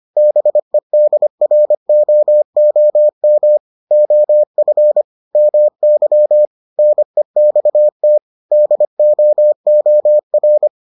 Morse code, heard as a keyed tone, about 25 wpm, 600 Hz. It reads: BEDROOM OF MY NEXT DOOR